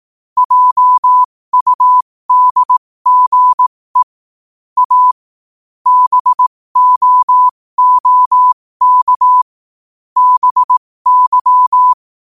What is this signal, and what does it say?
Morse code, heard as a keyed tone, about 18 wpm, spaced as for 15 wpm, 1000 Hz. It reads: JUDGE A BOOK BY
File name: qotdfiles / 15wpm